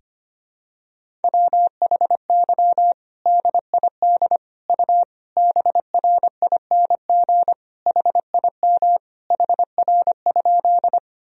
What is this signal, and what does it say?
Morse code, heard as a keyed tone, about 25 wpm, 700 Hz. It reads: WHY DID U BRING HIM HR?
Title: Sentences from Top 500 Words and Abbreviations 25wpm